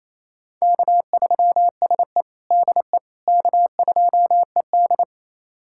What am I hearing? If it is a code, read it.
Morse code, heard as a keyed tone, about 28 wpm, 690 Hz. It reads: K3SE DE K2ED
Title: USA Callsign DE Callsign - 4 Characters - 28wpm